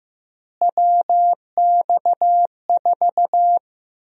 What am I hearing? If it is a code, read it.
Morse code, heard as a keyed tone, about 15 wpm, 700 Hz. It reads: WX4
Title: USA Call Sign Prefixes - Repeated 2x - 15wpm